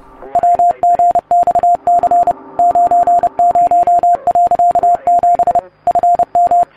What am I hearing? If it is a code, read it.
Morse code, heard as a keyed tone, about 30 wpm, 700 Hz. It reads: WGXC 90.7 FM